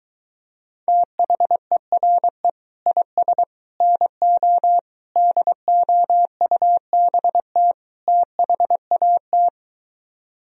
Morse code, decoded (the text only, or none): THERE IS NO DOUBT THAT